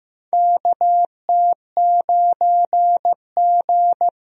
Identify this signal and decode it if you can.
Morse code, heard as a keyed tone, about 15 wpm, 700 Hz. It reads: KT9G
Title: USA Call Signs - 4 Characters - Repeated 2x - 15wpm